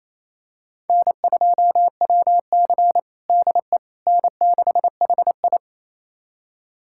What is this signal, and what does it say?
Morse code, heard as a keyed tone, about 28 wpm, 700 Hz. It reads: N2WC DE N6HI
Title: USA Callsign DE Callsign - 4 Characters - 28wpm